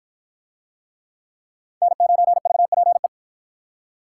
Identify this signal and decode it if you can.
Morse code, heard as a keyed tone, about 51 wpm, 700 Hz. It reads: N9VPE